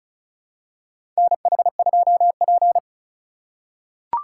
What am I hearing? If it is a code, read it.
Morse code, heard as a keyed tone, about 34 wpm, 700 Hz. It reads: NH2P
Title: USA Call Signs - 4 Characters - 4x Character Spacing - 35wpm